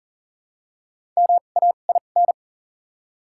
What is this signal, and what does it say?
Morse code, heard as a keyed tone, about 40 wpm, spaced as for 29 wpm, 700 Hz. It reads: MAIN
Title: 2x Character Spacing - Top 500 Words - 40wpm